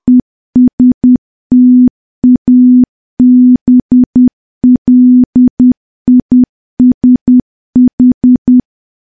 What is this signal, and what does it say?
Morse code, heard as a keyed tone, about 10 wpm, 250 Hz. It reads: ESTABLISH
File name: morse / morse-training-10WPM